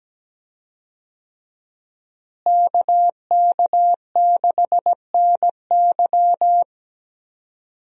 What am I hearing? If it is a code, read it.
Morse code, heard as a keyed tone, about 17 wpm, 700 Hz. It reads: KK6NY
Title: USA Call Signs - 5 Characters - Repeated 2x - 17wpm